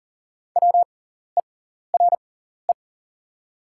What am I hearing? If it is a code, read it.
Morse code, heard as a keyed tone, about 40 wpm, spaced as for 14 wpm, 700 Hz. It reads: WERE
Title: 6x Character Spacing - Top 200 Words - 40wpm